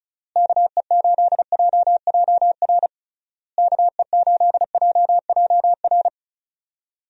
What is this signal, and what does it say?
Morse code, heard as a keyed tone, about 36 wpm, 700 Hz. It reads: KE8JJR KE8JJR
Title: USA Call Signs - 6 Characters - Repeated 3x - 35wpm